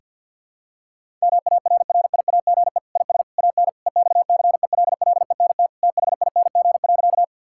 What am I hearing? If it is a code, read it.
Morse code, heard as a keyed tone, about 46 wpm, 700 Hz. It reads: MARRIAGE IS AN EXCELLENT THING.
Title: Sentences from Top 1000 Words - Review Part 5 - 50wpm